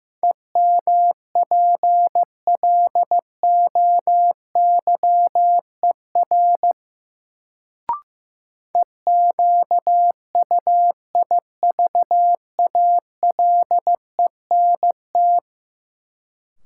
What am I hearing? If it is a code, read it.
Morse code, heard as a keyed tone, about 15 wpm, 700 Hz. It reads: EMPLOYER EQUIVALENT
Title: Sentences from Top 1600 Words - New Word Drill - 15wpm